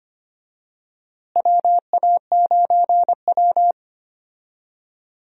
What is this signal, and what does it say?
Morse code, heard as a keyed tone, about 25 wpm, 700 Hz. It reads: WA9W
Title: USA Call Signs - 4 Characters - Repeated 3x - 25wpm